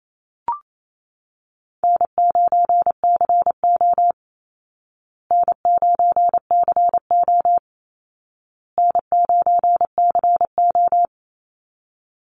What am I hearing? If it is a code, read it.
Morse code, heard as a keyed tone, about 28 wpm, 700 Hz. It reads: N9CO N9CO N9CO
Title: Easy Call Signs Repeated 3x - 28wpm